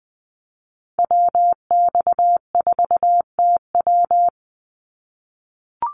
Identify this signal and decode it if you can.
Morse code, heard as a keyed tone, about 20 wpm, 700 Hz. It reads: WX4TW
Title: NA SOTA Activator Callsigns 20wpm